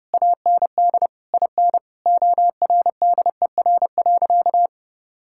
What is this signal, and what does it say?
Morse code, heard as a keyed tone, about 30 wpm, 700 Hz. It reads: AND IN ORDER.
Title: Sentences from Top 1000 Words - Review Part 6 - 30wpm